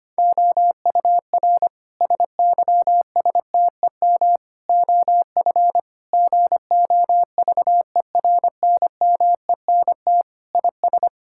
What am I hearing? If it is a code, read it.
Morse code, heard as a keyed tone, about 25 wpm, 700 Hz. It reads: OUR SYSTEM OF GOVERNMENT IS
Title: Sentences from Top 1000 Words - Review Part 6 - 25wpm